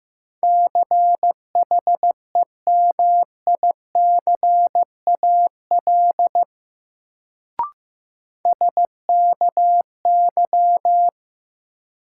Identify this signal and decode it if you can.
Morse code, heard as a keyed tone, about 16 wpm, 700 Hz. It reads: CHEMICAL SKY
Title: Sentences from Top 1200 Words - New Word Drill - 15wpm